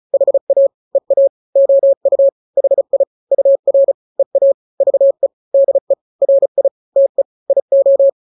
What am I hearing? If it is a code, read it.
Morse code, heard as a keyed tone, about 35 wpm, 550 Hz. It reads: HA EA OU HI UR EA VE DE RI TE IO
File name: n-grams / 2-grams 35 30 WPM